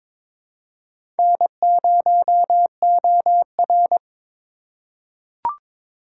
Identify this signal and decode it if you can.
Morse code, heard as a keyed tone, about 22 wpm, 700 Hz. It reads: N0OR
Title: USA Call Signs - 8x Character Spacing - 22wpm